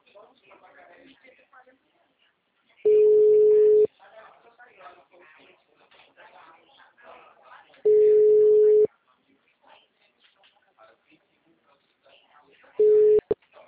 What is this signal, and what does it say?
Ringback tone (Europe, 425 Hz), with speech.